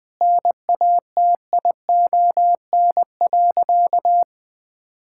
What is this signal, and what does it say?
Morse code, heard as a keyed tone, about 20 wpm, 700 Hz. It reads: NATION.
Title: Sentences with 5-Word Phrases - Spoken Beginning - 20wpm